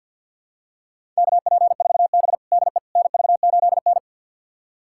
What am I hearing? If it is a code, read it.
Morse code, heard as a keyed tone, about 47 wpm, 700 Hz. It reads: KP4B DE NV8N
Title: USA Callsign DE Callsign - 4 Characters - 50wpm